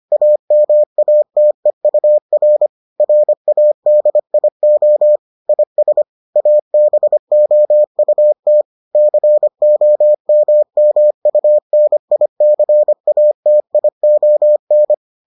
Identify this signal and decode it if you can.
Morse code, heard as a keyed tone, about 25 wpm, 600 Hz. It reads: AMATEUR RADIO IS ABOUT COMMUNICATION